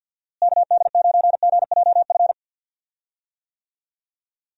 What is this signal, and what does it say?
Morse code, heard as a keyed tone, about 51 wpm, 700 Hz. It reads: KD9GJF
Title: USA Call Signs - 6 Characters - Repeated 2x - 50wpm